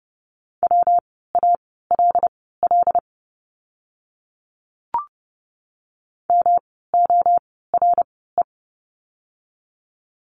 Morse code, heard as a keyed tone, about 30 wpm, spaced as for 17 wpm, 700 Hz. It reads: WALL MORE
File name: course / 30:10wpm / 115 Lesson 19f - Letter w - Words 30wpm